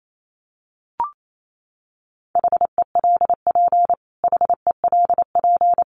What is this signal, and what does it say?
Morse code, heard as a keyed tone, about 28 wpm, 700 Hz. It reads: HELP HELP
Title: Top 200 Words Repeated 28wpm